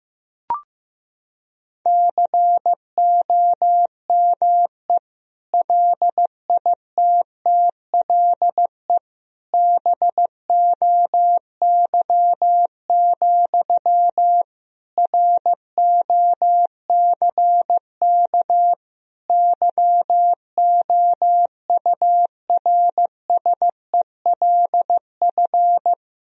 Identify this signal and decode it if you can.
Morse code, heard as a keyed tone, about 15 wpm, 700 Hz. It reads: COME LITTLE BOY, ROCK YOURSELF
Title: Sentences from Top 1000 Words - Review Part 2 - 15wpm